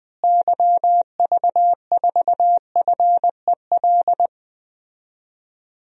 Morse code, heard as a keyed tone, about 20 wpm, 700 Hz. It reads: YV4FEL
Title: DXCC-100 Call Signs - 6 Characters - Repeated 3x - 20wpm